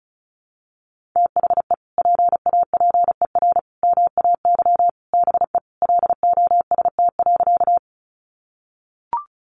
Morse code, heard as a keyed tone, about 34 wpm, 700 Hz. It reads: THE PAPER MAY BE LOST.
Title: Sentences from Top 1400 Words 35wpm